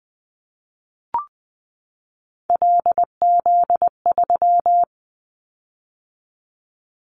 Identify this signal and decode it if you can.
Morse code, heard as a keyed tone, about 20 wpm, 700 Hz. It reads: LZ3
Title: Callsign Prefixes for Top 100 DXCC - 20wpm